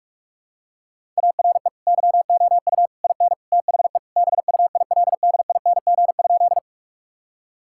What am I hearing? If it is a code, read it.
Morse code, heard as a keyed tone, about 46 wpm, 700 Hz. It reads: ARE YOU IN THE BUILDING?
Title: Sentences from Top 1000 Words - Review Part 5 - 45wpm